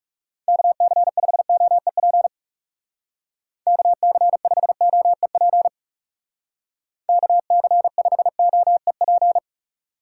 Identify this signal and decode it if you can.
Morse code, heard as a keyed tone, about 40 wpm, 700 Hz. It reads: KC5OEP KC5OEP KC5OEP